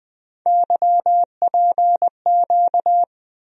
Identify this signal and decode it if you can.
Morse code, heard as a keyed tone, about 20 wpm, 700 Hz. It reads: YPQ
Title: USA Call Sign Postfixes - 3 Letters 20wpm